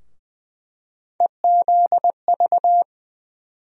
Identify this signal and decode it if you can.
Morse code, heard as a keyed tone, about 20 wpm, 700 Hz. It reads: EZV
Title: USA Call Sign Postfixes - 3 Letters 20wpm